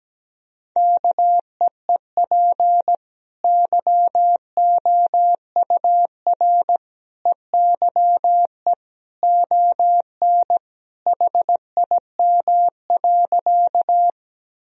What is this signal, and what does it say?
Morse code, heard as a keyed tone, about 17 wpm, 700 Hz. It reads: KEEP YOUR EYE ON HIM.
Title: Sentences from Top 1000 Words - Review Part 1 - 17wpm